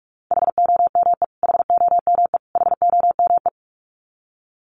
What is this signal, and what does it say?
Morse code, heard as a keyed tone, about 44 wpm, 700 Hz. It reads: HOME HOME HOME
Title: Top 300 Words Repeated 45wpm